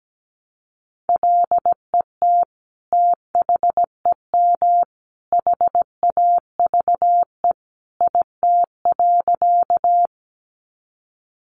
Morse code, heard as a keyed tone, about 17 wpm, 700 Hz. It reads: LET THEM HAVE IT.
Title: Sentences from Top 300 Words 17wpm